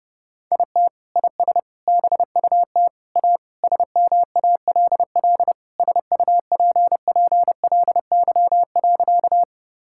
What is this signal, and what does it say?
Morse code, heard as a keyed tone, about 32 wpm, 700 Hz. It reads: IT IS BUT A SMALL SUPPLY.
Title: Sentences from Top 1000 Words - Review Part 5 - 30wpm